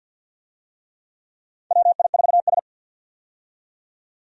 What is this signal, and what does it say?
Morse code, heard as a keyed tone, about 53 wpm, 700 Hz. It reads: WI4S
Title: USA Call Signs - 4 Characters - 50wpm